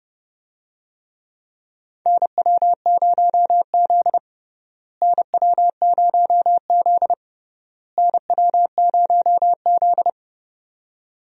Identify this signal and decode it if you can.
Morse code, heard as a keyed tone, about 32 wpm, 700 Hz. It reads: NW0Z NW0Z NW0Z
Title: USA Call Signs - 4 Characters - Repeated 3x - 30wpm